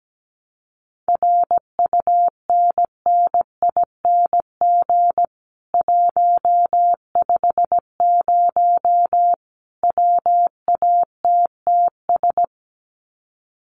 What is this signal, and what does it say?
Morse code, heard as a keyed tone, about 17 wpm, 700 Hz. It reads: RUNNING 150 WATTS
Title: Common QSO Phrases 17wpm